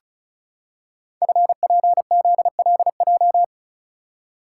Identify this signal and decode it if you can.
Morse code, heard as a keyed tone, about 36 wpm, 700 Hz. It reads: FPZLJ
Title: Code Groups - 5 Letter:Numbers - 35wpm